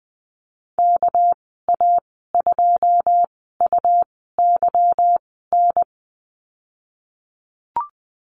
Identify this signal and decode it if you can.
Morse code, heard as a keyed tone, about 20 wpm, 700 Hz. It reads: K A 2 U Y N
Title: 238 Lesson 38g - Letter q - Call Signs 20wpm